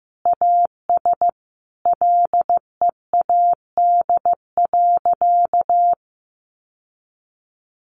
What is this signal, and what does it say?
Morse code, heard as a keyed tone, about 15 wpm, 700 Hz. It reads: AS LEAD.
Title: Sentences from Top 1100 Words 15wpm